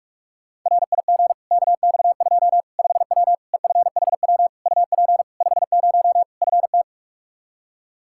Morse code, heard as a keyed tone, about 46 wpm, 700 Hz. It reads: RIG KX2 5W EFHW UP 50 FT
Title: Practice QSO - Exchange 2 - Info - 45wpm